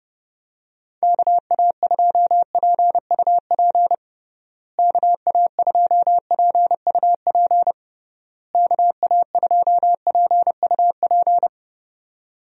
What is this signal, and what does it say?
Morse code, heard as a keyed tone, about 32 wpm, 700 Hz. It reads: KA2PUP KA2PUP KA2PUP
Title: USA Call Signs - 6 Characters - Repeated 3x - 30wpm